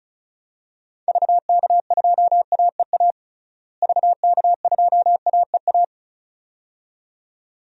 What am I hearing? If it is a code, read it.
Morse code, heard as a keyed tone, about 34 wpm, 700 Hz. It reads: VK2AEA VK2AEA
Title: DXCC-100 Call Signs - 6 Characters - Repeated 2x - 35wpm